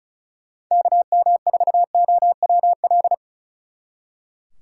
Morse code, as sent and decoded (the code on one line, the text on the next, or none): -.- -- ....- --- .-- .-..
KM4OWL